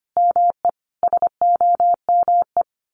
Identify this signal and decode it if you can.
Morse code, heard as a keyed tone, about 25 wpm, 700 Hz. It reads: ME SOME
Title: Sentences from Top 1500 Words 25wpm